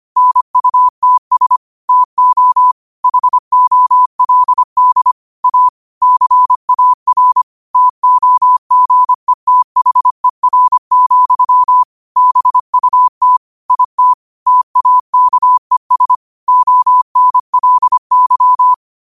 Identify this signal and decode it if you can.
Morse code, heard as a keyed tone, about 25 wpm, 1000 Hz. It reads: NUTS TO HOLD A CAR TOGETHER, BUT IT TAKES ONLY
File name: qotdfiles / 25wpm